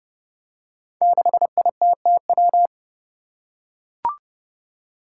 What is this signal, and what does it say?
Morse code, heard as a keyed tone, about 30 wpm, 700 Hz. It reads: 6ITTW
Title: Five Challenging Letter-Number Combinations 30wpm